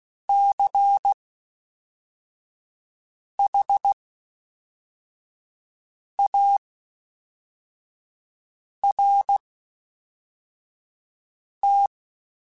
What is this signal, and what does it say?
Morse code, heard as a keyed tone, about 16 wpm, spaced as for 4 wpm, 800 Hz. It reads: CHART
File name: Special INGs / Special INGs-mp3s / Fast ing @10